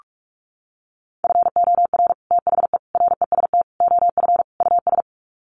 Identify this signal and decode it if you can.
Morse code, heard as a keyed tone, about 44 wpm, 700 Hz. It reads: FOR THE REST OF US
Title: Sets of 5 Words Encore 45wpm